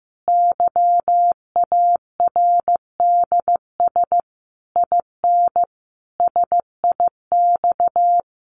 Morse code, heard as a keyed tone, about 15 wpm, 700 Hz. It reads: YARDS IN SIX
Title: A Sound Of Thunder 15wpm